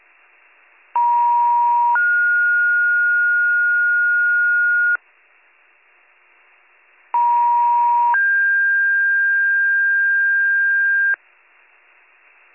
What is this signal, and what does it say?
Two-tone page, heard as a one-second tone, then a three-second tone, about 950 then 1430 Hz, about 950 then 1600 Hz. It is heard from 0.9 to 11.1 s.